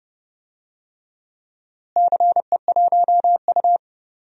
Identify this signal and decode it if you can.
Morse code, heard as a keyed tone, about 32 wpm, 700 Hz. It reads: CE1U
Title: DXCC-100 Call Signs - 4 Characters - Repeated 3x - 30wpm